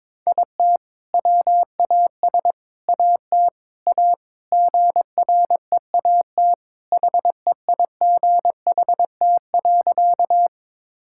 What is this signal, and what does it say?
Morse code, heard as a keyed tone, about 22 wpm, 700 Hz. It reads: IT WAS AT A GREAT HEIGHT.
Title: Sentences from Top 1600 Words 22wpm